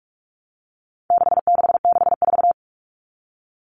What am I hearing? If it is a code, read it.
Morse code, heard as a keyed tone, about 44 wpm, 700 Hz. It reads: 6664
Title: Numbers - 4-Digit 45wpm